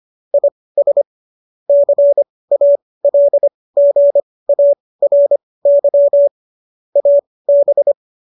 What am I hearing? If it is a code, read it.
Morse code, heard as a keyed tone, about 25 wpm, spaced as for 18 wpm, 570 Hz. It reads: IS CALGARY AB
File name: qso203 18